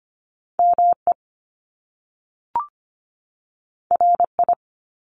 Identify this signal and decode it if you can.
Morse code, heard as a keyed tone, about 25 wpm, 700 Hz. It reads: ME RI
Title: Top Two Letter Combinations in Words 25wpm